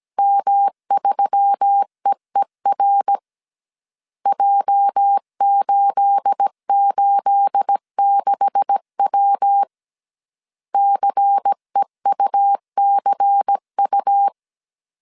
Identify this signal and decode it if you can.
Morse code, heard as a keyed tone, about 17 wpm, 800 Hz. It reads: M3EER J886W CEUCU
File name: CW / DET / 1 9 1500C S A 16wpm 2828